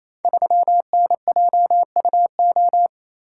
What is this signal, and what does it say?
Morse code, heard as a keyed tone, about 28 wpm, 700 Hz. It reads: 3NJUO
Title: Code Groups - 5 Letter:Numbers - 28wpm